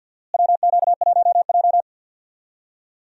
Morse code, heard as a keyed tone, about 46 wpm, 700 Hz. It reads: WQ1J
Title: USA Call Signs - 4 Characters - Repeated 3x - 50wpm